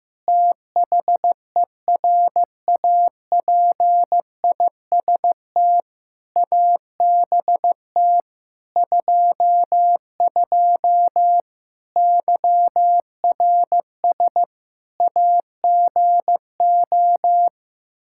Morse code, heard as a keyed tone, about 16 wpm, 700 Hz. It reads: THERAPIST ABT 22 YRS AGO
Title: Practice QSO - Exchange 3 - More Info - 15wpm